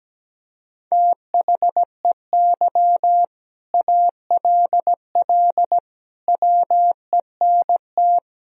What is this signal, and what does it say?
Morse code, heard as a keyed tone, about 17 wpm, 700 Hz. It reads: THEY ALL WENT
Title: Sentences from Top 1000 Words - Review Part 1 - 17wpm